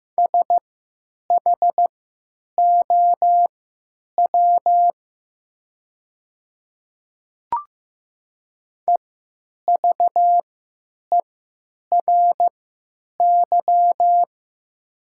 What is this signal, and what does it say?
Morse code, heard as a keyed tone, about 15 wpm, spaced as for 9 wpm, 700 Hz. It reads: SHOW EVERY
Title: 3x Character Spacing - Top 300 Words - 15wpm